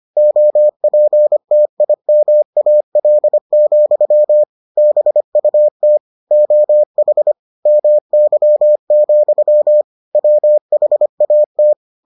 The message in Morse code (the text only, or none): OPTIMAL, BUT OH MY, WHAT